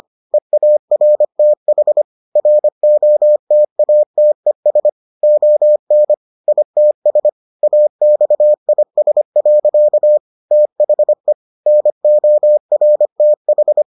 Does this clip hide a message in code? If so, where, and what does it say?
Yes: Morse code from 0.3 to 13.8 s, heard as a keyed tone, about 25 wpm, 600 Hz. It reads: EARTH ROTATES ON ITS AXIS. THE NORTH